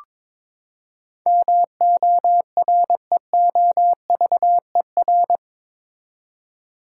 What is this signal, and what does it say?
Morse code, heard as a keyed tone, about 22 wpm, 700 Hz. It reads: MOREOVER
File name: practice / 22wpm / Eight Letter Words - 22wpm